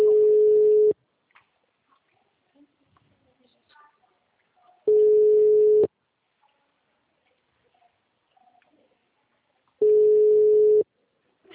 Ringback tone (Europe, 425 Hz).